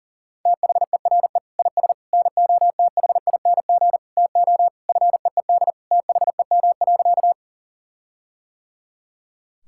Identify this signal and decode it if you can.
Morse code, heard as a keyed tone, about 40 wpm, 700 Hz. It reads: THERE IS NOTHING TO FEED THEM.
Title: Sentences from Top 1600 Words 40wpm